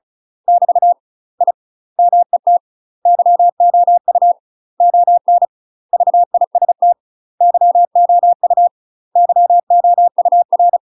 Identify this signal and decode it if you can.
Morse code, heard as a keyed tone, about 36 wpm, 700 Hz. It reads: <BT> I MET YOU ON VIST YOU YOUR
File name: UK MASTER QSOs TRAINING / 35WPM / UK MASTER QSOs 0032